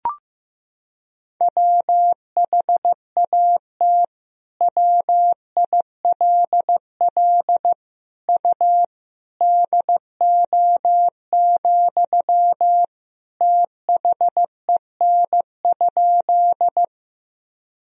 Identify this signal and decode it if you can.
Morse code, heard as a keyed tone, about 15 wpm, 700 Hz. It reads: WHAT WILL U DO, THEN?
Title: Sentences from Top 100 Words and Abbreviations 15wpm